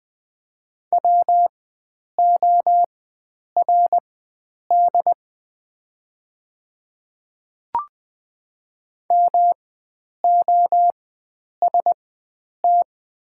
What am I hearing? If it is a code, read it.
Morse code, heard as a keyed tone, about 20 wpm, spaced as for 9 wpm, 690 Hz. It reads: WORD MOST
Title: 4x Character Spacing - Top 100 Words - 20wpm